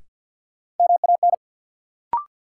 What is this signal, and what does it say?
Morse code, heard as a keyed tone, about 52 wpm, 700 Hz. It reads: MAN
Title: Sentences from Top 200 Words - New Word Drill - 50wpm